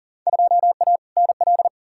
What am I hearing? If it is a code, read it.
Morse code, heard as a keyed tone, about 40 wpm, 700 Hz. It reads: 2A NL
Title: Field Day Exchanges Encore - 40wpm